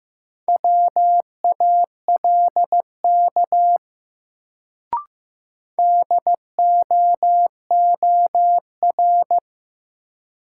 Morse code, heard as a keyed tone, about 15 wpm, 700 Hz. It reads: WALK DOOR
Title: Sentences from Top 300 Words - New Word Drill - 15wpm